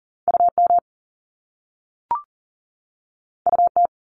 Morse code, heard as a keyed tone, about 41 wpm, 700 Hz. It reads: UM UT
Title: Top Two Letter Combinations in Words 40wpm